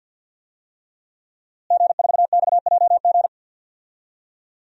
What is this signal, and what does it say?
Morse code, heard as a keyed tone, about 50 wpm, 700 Hz. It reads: G4XJG
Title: DXCC-100 Call Signs - 5 Characters - Repeated 3x - 50wpm